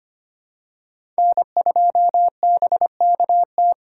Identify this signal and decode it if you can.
Morse code, heard as a keyed tone, about 25 wpm, 700 Hz. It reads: N2BKT
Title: USA Call Signs - 5 Characters - 4x Character Spacing - 25wpm